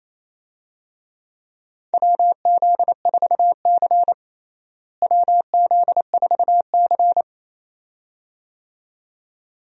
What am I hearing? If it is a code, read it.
Morse code, heard as a keyed tone, about 28 wpm, 700 Hz. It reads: WZ4C WZ4C
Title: USA Call Signs - 4 Characters - Repeated 2x - 28wpm